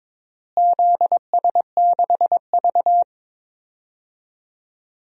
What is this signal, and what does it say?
Morse code, heard as a keyed tone, about 21 wpm, 700 Hz. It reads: ZS6V
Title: DXCC-100 Call Signs - 4 Characters - Repeated 3x - 22wpm